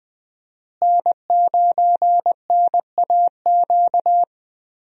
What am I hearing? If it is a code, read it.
Morse code, heard as a keyed tone, about 20 wpm, 700 Hz. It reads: N9NAQ
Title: USA Call Signs - 5 Characters - Repeated 3x - 20wpm